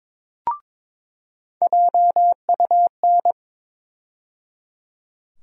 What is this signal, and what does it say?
Morse code, heard as a keyed tone, about 22 wpm, 700 Hz. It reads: JUN
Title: USA Call Sign Postfixes - 3 Letters 22wpm